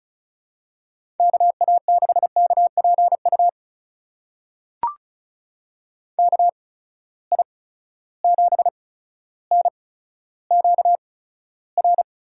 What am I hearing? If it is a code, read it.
Morse code, heard as a keyed tone, about 36 wpm, spaced as for 15 wpm, 700 Hz. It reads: KA6KPU KI7NQR
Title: USA Call Signs - 6 Characters - 8x Character Spacing - 35wpm